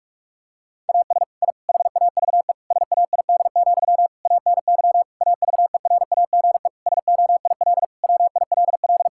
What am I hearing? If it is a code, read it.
Morse code, heard as a keyed tone, about 46 wpm, 700 Hz. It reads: AS I HAVE SAID, ANY AVERAGE SOIL WILL
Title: Sentences from Top 1000 Words - Review Part 6 - 45wpm